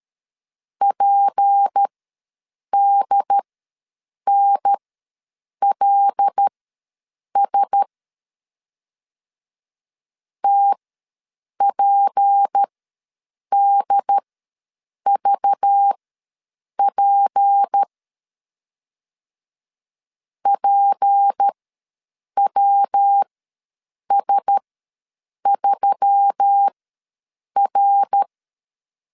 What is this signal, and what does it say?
Morse code, heard as a keyed tone, about 13 wpm, spaced as for 7 wpm, 800 Hz. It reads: PDNLS TPDVP PWS3R
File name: LEZIONI / 6 PKX0 / 6 P ws 12wpm 3117